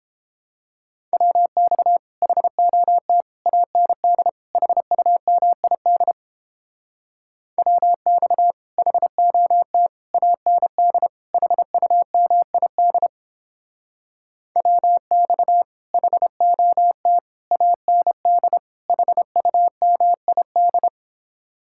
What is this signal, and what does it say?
Morse code, heard as a keyed tone, about 29 wpm, 700 Hz. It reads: WX HOT AND HUMID WX HOT AND HUMID WX HOT AND HUMID